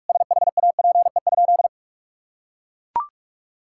Morse code, heard as a keyed tone, about 41 wpm, 700 Hz. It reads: SHAPE?
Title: Sentences from Top 1000 Words - Review Part 6 - 45wpm